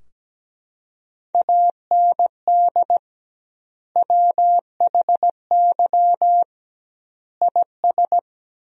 Morse code, heard as a keyed tone, about 17 wpm, 700 Hz. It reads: AND WHY IS
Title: 2x Word-Spacing - Sentences from Top 200 Words - 17wpm